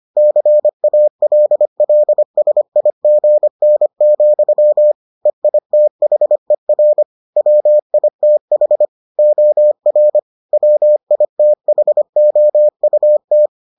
Morse code, heard as a keyed tone, about 25 wpm, 600 Hz. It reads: CALLSIGN, EITHER WITH OR WITHOUT